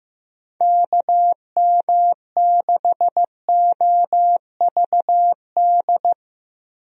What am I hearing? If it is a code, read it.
Morse code, heard as a keyed tone, about 16 wpm, 700 Hz. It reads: KM6OVD
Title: USA Call Signs - 6 Characters - 15wpm